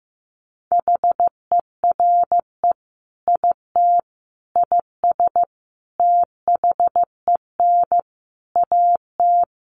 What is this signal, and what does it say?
Morse code, heard as a keyed tone, about 15 wpm, 700 Hz. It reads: HERE IT IS THEN AT
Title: Sentences from Top 1500 Words 15wpm